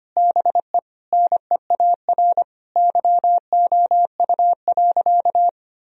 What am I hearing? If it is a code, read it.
Morse code, heard as a keyed tone, about 25 wpm, 700 Hz. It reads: BE NEAR YOU.
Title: Sentences from Top 1000 Words - Review Part 1 - 25wpm